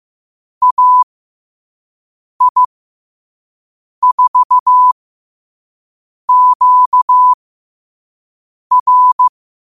Morse code, heard as a keyed tone, about 15 wpm, spaced as for 5 wpm, 1000 Hz. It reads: AI4QR
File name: qotdfiles / 5wpm